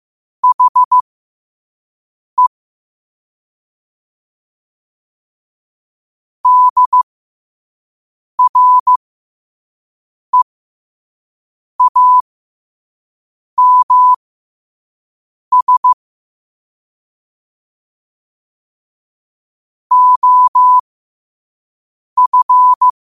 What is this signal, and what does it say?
Morse code, heard as a keyed tone, about 15 wpm, spaced as for 5 wpm, 1000 Hz. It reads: HE DREAMS OF
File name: qotdfiles / 5wpm